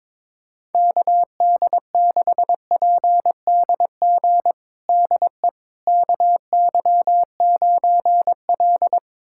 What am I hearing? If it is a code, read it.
Morse code, heard as a keyed tone, about 22 wpm, 700 Hz. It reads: KD6PDG DE KY9L
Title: USA Callsign DE Callsign 22wpm